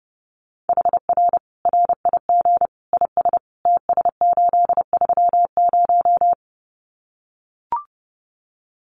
Morse code, heard as a keyed tone, about 30 wpm, 700 Hz. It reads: HR RIG IS TS830
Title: Common QSO Phrases 30wpm